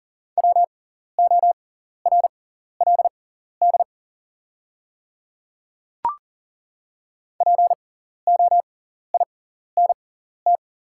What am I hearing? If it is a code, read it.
Morse code, heard as a keyed tone, about 40 wpm, spaced as for 14 wpm, 700 Hz. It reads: WORLD POINT
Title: 6x Character Spacing - Top 200 Words - 40wpm